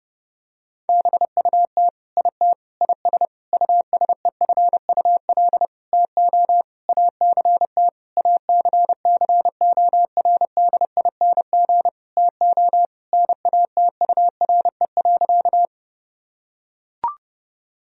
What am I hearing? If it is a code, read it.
Morse code, heard as a keyed tone, about 30 wpm, 700 Hz. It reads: BUT IT IS USEFUL TO ACT ACCORDING TO NATURE.
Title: Sentences from Top 1000 Words - Review Part 4 - 30wpm